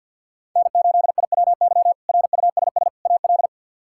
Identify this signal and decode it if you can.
Morse code, heard as a keyed tone, about 47 wpm, 700 Hz. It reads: N8IWY RUSS AL
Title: K1USN SST QSO Practice - 50wpm